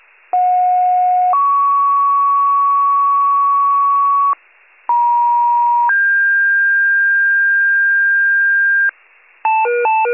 Two-tone page, heard as a one-second tone, then a three-second tone, about 730 then 1120 Hz, about 950 then 1600 Hz, with noise. It is heard from 0.3 to 8.9 s.